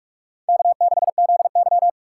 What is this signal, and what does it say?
Morse code, heard as a keyed tone, about 46 wpm, 700 Hz. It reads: K6ZY
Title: USA Call Signs - 4 Characters - 4x Character Spacing - 45wpm